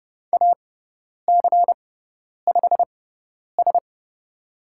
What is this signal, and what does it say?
Morse code, heard as a keyed tone, about 31 wpm, spaced as for 10 wpm, 700 Hz. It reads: AC5S
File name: Lesson 16g - Letter c - Call Signs 30wpm